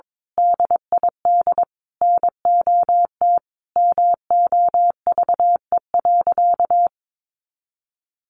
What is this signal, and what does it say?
Morse code, heard as a keyed tone, about 22 wpm, 700 Hz. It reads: DID NOT MOVE.
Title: A Sound Of Thunder 22wpm